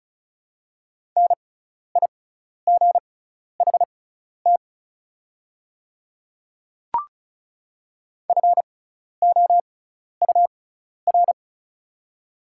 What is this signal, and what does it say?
Morse code, heard as a keyed tone, about 34 wpm, spaced as for 12 wpm, 700 Hz. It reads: NIGHT FOUR
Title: 6x Character Spacing - Top 300 Words - 35wpm